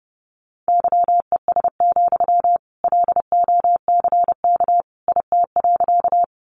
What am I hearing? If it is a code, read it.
Morse code, heard as a keyed tone, about 30 wpm, 700 Hz. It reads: YES, LOCK IT.
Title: Sentences from Top 1300 Words 30wpm